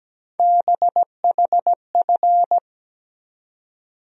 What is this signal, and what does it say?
Morse code, heard as a keyed tone, about 17 wpm, 700 Hz. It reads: BHF
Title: USA Call Sign Postfixes - 3 Letters 17wpm